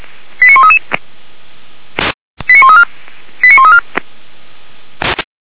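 Five-tone selective call (ZVEI), heard as quick runs of five tones, tones of about 70 ms. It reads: 89133 89134 89134
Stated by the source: ZVEI five-tone selective call received on SDR.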